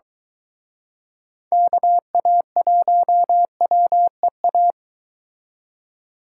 Morse code, heard as a keyed tone, about 23 wpm, 700 Hz. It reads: KA1WEA